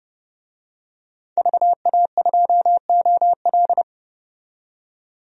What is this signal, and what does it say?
Morse code, heard as a keyed tone, about 30 wpm, 700 Hz. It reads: VA2OL
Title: DXCC-100 Call Signs - 5 Characters - 30wpm